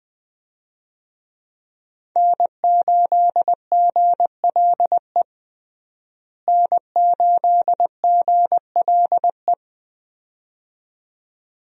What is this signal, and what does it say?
Morse code, heard as a keyed tone, about 20 wpm, 700 Hz. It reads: N8GLE N8GLE
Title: USA Call Signs - 5 Characters - Repeated 2x - 20wpm